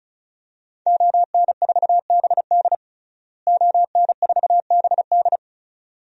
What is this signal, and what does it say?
Morse code, heard as a keyed tone, about 36 wpm, 700 Hz. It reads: ON4BD ON4BD
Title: DXCC-100 Call Signs - 5 Characters - Repeated 2x - 35wpm